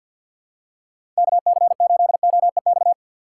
Morse code, heard as a keyed tone, about 52 wpm, 700 Hz. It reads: KC8OEX
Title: USA Call Signs - 6 Characters - 8x Character Spacing - 50wpm